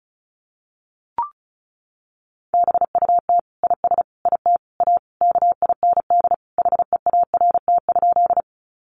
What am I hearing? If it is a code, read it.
Morse code, heard as a keyed tone, about 34 wpm, 700 Hz. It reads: BUT IS IT A KIND HEART?
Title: Sentences from Top 1200 Words 35wpm